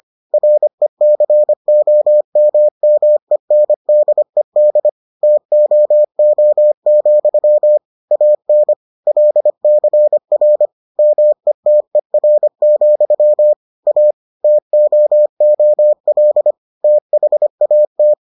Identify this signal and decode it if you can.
Morse code, heard as a keyed tone, about 25 wpm, 600 Hz. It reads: RECOMMENDED TOO, AN LCR METER, A TOOL THAT